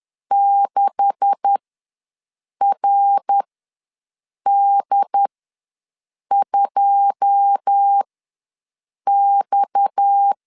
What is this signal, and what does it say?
Morse code, heard as a keyed tone, about 10 wpm, spaced as for 6 wpm, 800 Hz. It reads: 6RD2X
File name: LEZIONI / 10 SIGNS / 10A SLASH ws 10wpm 4121